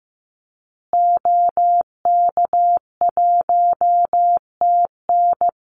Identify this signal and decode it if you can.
Morse code, heard as a keyed tone, about 15 wpm, 700 Hz. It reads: OK1TN
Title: Call Signs Repeated 2x - 15wpm